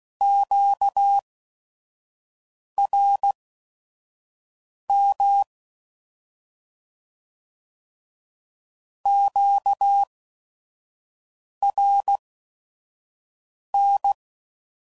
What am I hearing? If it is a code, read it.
Morse code, heard as a keyed tone, about 16 wpm, spaced as for 5 wpm, 800 Hz. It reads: QRM QRN
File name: Mixed / Misc / Words